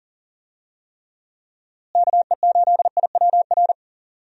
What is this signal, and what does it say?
Morse code, heard as a keyed tone, about 37 wpm, 700 Hz. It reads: KE8IWR
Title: USA Call Signs - 6 Characters - Repeated 2x - 40wpm